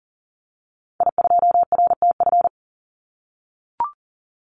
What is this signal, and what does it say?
Morse code, heard as a keyed tone, about 40 wpm, 700 Hz. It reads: I2RTF
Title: Call Signs - 40wpm